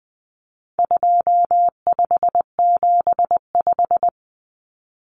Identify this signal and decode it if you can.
Morse code, heard as a keyed tone, about 19 wpm, 700 Hz. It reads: 2575
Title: Numbers - 4-Digit 20wpm